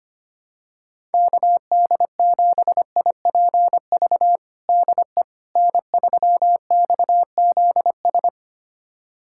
Morse code, heard as a keyed tone, about 25 wpm, 700 Hz. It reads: KD7IPV DE N3XZS
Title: USA Callsign DE Callsign 25wpm